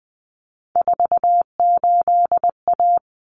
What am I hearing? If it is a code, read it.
Morse code, heard as a keyed tone, about 20 wpm, 700 Hz. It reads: 48A
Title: Common Cut Numbers - 3-Digit 20wpm